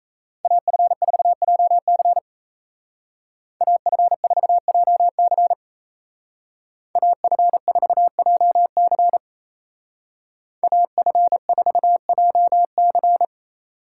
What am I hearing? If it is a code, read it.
Morse code, heard as a keyed tone, about 33 wpm, 700 Hz. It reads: AF4JC AF4JC AF4JC AF4JC
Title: Speed Racing - USA 2x2 Call Signs 28wpm